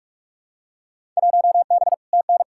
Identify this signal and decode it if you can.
Morse code, heard as a keyed tone, about 47 wpm, 700 Hz. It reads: 1B TN
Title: Field Day Exchanges - 45wpm